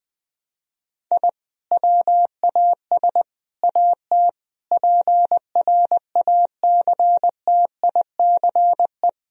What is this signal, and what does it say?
Morse code, heard as a keyed tone, about 20 wpm, 700 Hz. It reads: I WAS AT PRACTICE
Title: Sentences from Top 1000 Words - Review Part 4 - 20wpm